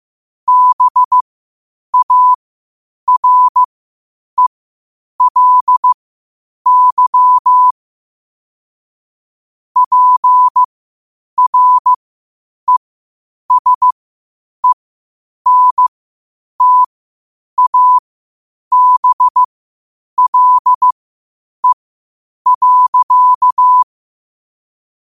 Morse code, heard as a keyed tone, about 15 wpm, spaced as for 8 wpm, 1000 Hz. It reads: BARELY PRESENTABLE.
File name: qotdfiles / 8wpm